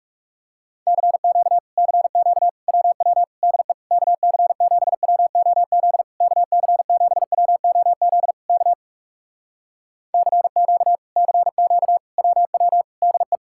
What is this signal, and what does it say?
Morse code, heard as a keyed tone, about 37 wpm, 700 Hz. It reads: CQ CQ WW DE KC7WOZ KC7WOZ K CQ CQ WW DE
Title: Speed-Racing Common QSO Phrases Encore - 30wpm